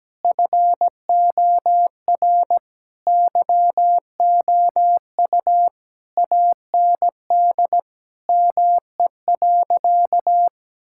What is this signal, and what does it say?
Morse code, heard as a keyed tone, about 17 wpm, 700 Hz. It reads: FOR YOU AND ME.
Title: Sentences from Top 1600 Words 17wpm